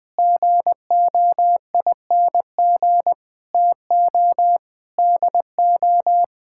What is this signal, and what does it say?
Morse code, heard as a keyed tone, about 20 wpm, 700 Hz. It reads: GOING TO DO
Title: Sentences with 5-Word Phrases - Spoken Beginning - 20wpm